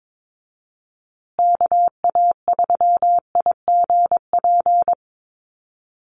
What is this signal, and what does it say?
Morse code, heard as a keyed tone, about 22 wpm, 700 Hz. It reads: KA3IGP
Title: USA Call Signs Repeated 2x 22wpm